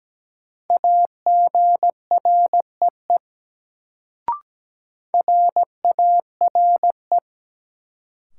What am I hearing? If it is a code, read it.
Morse code, heard as a keyed tone, about 17 wpm, 700 Hz. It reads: AGREE RARE
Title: Sentences from Top 1100 Words - New Word Drill Alt - 17wpm